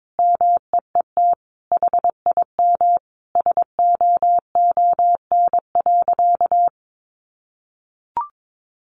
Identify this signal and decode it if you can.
Morse code, heard as a keyed tone, about 22 wpm, 700 Hz. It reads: MEET HIM SOON.
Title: Sentences from Top 1100 Words 22wpm